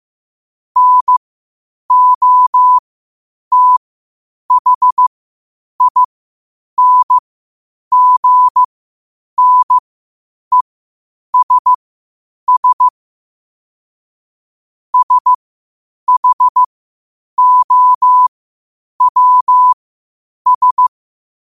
Morse code, heard as a keyed tone, about 15 wpm, spaced as for 8 wpm, 1000 Hz. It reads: NOTHINGNESS SHOWS
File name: qotdfiles / 8wpm